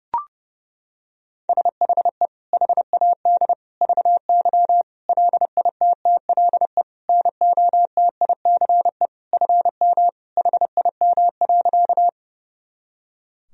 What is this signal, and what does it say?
Morse code, heard as a keyed tone, about 32 wpm, 700 Hz. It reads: SHE HAD VY LITTLE NOTICE FM HIM.
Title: Sentences from Top 400 Words and Abbreviations 30wpm